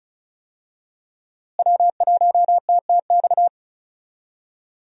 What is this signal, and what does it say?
Morse code, heard as a keyed tone, about 34 wpm, 700 Hz. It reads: W1TTX
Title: USA Call Signs - 5 Characters - Repeated 2x - 35wpm